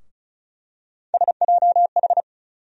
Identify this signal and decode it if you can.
Morse code, heard as a keyed tone, about 36 wpm, 700 Hz. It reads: SJH
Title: USA Call Sign Postfixes - 3 Letters 35wpm